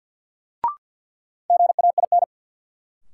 Morse code, heard as a keyed tone, about 52 wpm, 700 Hz. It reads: GAIN
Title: Sentences from Top 1000 Words - New Word Drill - 50wpm